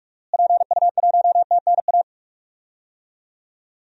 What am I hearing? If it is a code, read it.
Morse code, heard as a keyed tone, about 46 wpm, 700 Hz. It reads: PU1TNA
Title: DXCC-100 Call Signs - 6 Characters - Repeated 2x - 45wpm